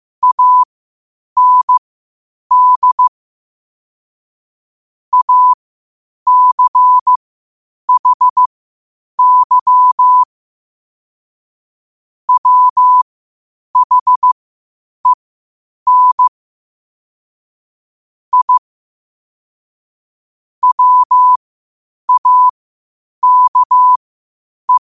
Morse code, heard as a keyed tone, about 15 wpm, spaced as for 8 wpm, 1000 Hz. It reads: AND ACHY WHEN I WAKE
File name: qotdfiles / 8wpm